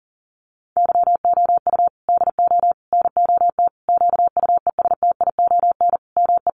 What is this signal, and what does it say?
Morse code, heard as a keyed tone, about 40 wpm, 700 Hz. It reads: YOU DO NOT QUESTION ME
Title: Sentences from Top 1100 Words 40wpm